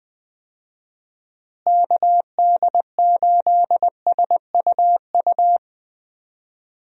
Morse code, heard as a keyed tone, about 20 wpm, 700 Hz. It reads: KD8SUU
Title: USA Call Signs - 6 Characters - Repeated 2x - 20wpm